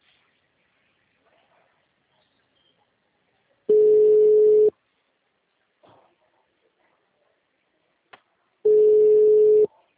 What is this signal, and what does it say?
Ringback tone (Europe, 425 Hz).
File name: out- -2019- - - .65672